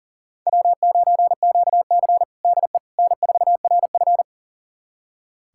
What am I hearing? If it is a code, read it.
Morse code, heard as a keyed tone, about 40 wpm, 700 Hz. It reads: W9QC DE N4RF
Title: USA Callsign DE Callsign - 4 Characters - 40wpm